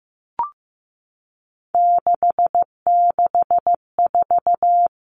Morse code, heard as a keyed tone, about 15 wpm, 700 Hz. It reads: 664
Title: Common Cut Numbers - 3-Digit 15wpm